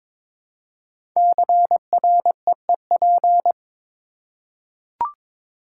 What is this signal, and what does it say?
Morse code, heard as a keyed tone, about 22 wpm, 700 Hz. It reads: CREEP
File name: 5 Letter Words - First 3 Letters Given - 22wpm